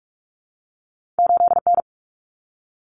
Morse code, heard as a keyed tone, about 46 wpm, 700 Hz. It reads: 8N
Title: Common Cut Numbers - 2-Digit 45wpm